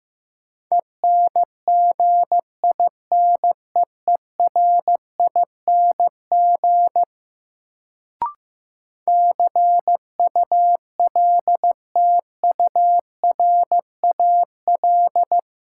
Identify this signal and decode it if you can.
Morse code, heard as a keyed tone, about 15 wpm, 700 Hz. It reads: ENGINEERING CULTURAL
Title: Sentences from Top 1500 Words - New Word Drill - 15wpm